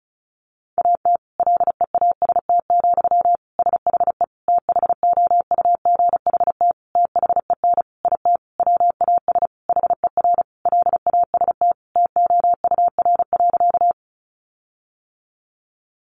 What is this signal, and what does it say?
Morse code, heard as a keyed tone, about 34 wpm, 700 Hz. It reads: AT LEAST, SHE THOUGHT THEN IT WAS HER LAST TOUR.
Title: Sentences from Top 1400 Words 35wpm